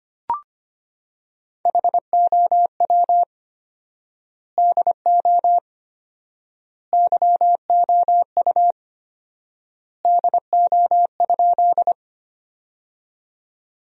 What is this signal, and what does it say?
Morse code, heard as a keyed tone, about 25 wpm, 700 Hz. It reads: HOW DO YOU DO?
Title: 4x Word-Spacing - Sentences from Top 100 Words - 25wpm